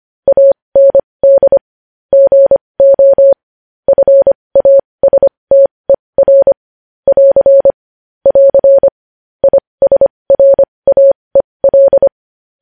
Morse code, heard as a keyed tone, about 25 wpm, spaced as for 20 wpm, 550 Hz. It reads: AND GO FASTER <AR> <AR> ISRAEL
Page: News Headlines in Morse Code at 20 WPM